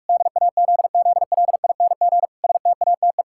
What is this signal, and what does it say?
Morse code, heard as a keyed tone, about 46 wpm, 700 Hz. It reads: DAZZLING STATE
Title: Pangram Challenge v2 - 45wpm